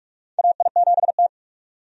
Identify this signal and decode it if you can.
Morse code, heard as a keyed tone, about 46 wpm, 700 Hz. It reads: AI7T
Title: USA Call Signs - 4 Characters - 45wpm